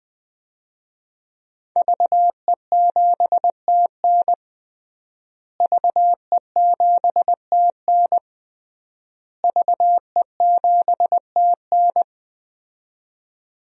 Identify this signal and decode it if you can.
Morse code, heard as a keyed tone, about 20 wpm, 700 Hz. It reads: VE7TN VE7TN VE7TN
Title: DXCC-100 Call Signs - 5 Characters - Repeated 3x - 20wpm